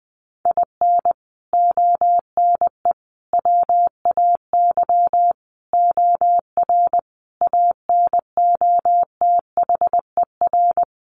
Morse code, heard as a keyed tone, about 20 wpm, 700 Hz. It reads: IN ONE WAY OR ANOTHER
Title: Sets of 5 Words Encore 20wpm